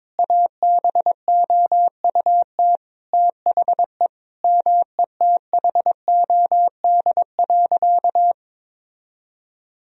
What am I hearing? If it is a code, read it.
Morse code, heard as a keyed tone, about 22 wpm, 700 Hz. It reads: ABOUT THE METHOD.
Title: Sentences from Top 1000 Words - Review Part 3 - 22wpm